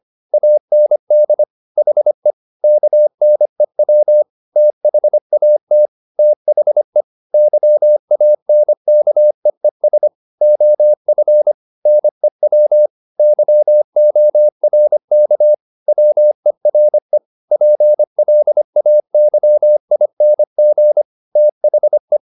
Morse code, heard as a keyed tone, about 25 wpm, 600 Hz. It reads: AND HE KNEW THAT THE YANKEES OF NEW YORK WERE PLAYING THE
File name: the old man and the sea part 1 25wpm